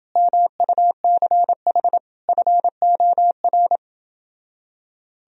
Morse code, heard as a keyed tone, about 29 wpm, 700 Hz. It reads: MUCH FOR